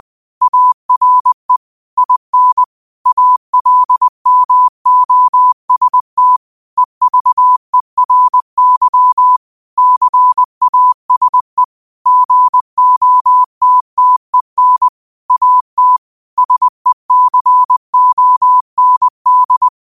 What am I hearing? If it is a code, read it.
Morse code, heard as a keyed tone, about 20 wpm, 1000 Hz. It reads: ARE IN ALMOST EVERY CASE GOTTEN AT SECOND